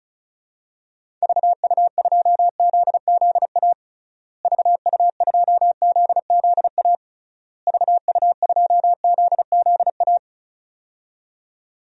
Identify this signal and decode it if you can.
Morse code, heard as a keyed tone, about 34 wpm, 700 Hz. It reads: VU2ZZA VU2ZZA VU2ZZA
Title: DXCC-100 Call Signs - 6 Characters - Repeated 3x - 35wpm